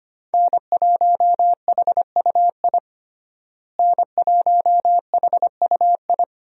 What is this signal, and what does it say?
Morse code, heard as a keyed tone, about 25 wpm, 700 Hz. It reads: N1HUI N1HUI
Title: USA Call Signs - 5 Characters - Repeated 3x - 25wpm